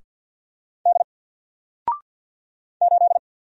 Morse code, heard as a keyed tone, about 49 wpm, 700 Hz. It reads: D 8
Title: Single Letter-Number Alt 50wpm